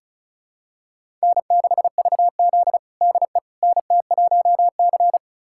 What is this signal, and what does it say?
Morse code, heard as a keyed tone, about 36 wpm, 700 Hz. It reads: N6VZ DE NT1C
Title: USA Callsign DE Callsign - 4 Characters - 35wpm